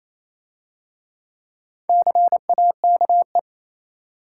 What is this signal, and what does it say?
Morse code, heard as a keyed tone, about 27 wpm, 690 Hz. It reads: CAKE
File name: Easy Words - Speed-Racing with Hints - 22wpm